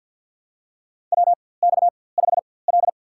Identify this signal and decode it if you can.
Morse code, heard as a keyed tone, about 52 wpm, spaced as for 23 wpm, 700 Hz. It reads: WX5L
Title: USA Call Signs - 4x Character Spacing - 50wpm